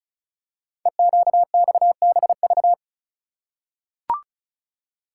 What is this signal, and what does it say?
Morse code, heard as a keyed tone, about 36 wpm, 700 Hz. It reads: EQXBV
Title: Code Groups - 5 Letter:Numbers - 35wpm